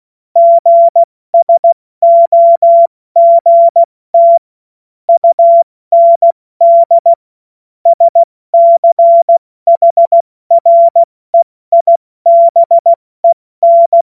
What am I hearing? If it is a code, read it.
Morse code, heard as a keyed tone, about 16 wpm, 680 Hz. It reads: GSOGT UND SCHREIBEN
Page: cw-witze